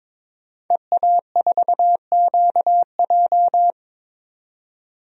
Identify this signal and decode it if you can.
Morse code, heard as a keyed tone, about 22 wpm, 700 Hz. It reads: EA4QJ
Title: DXCC-100 Call Signs - 5 Characters - 22wpm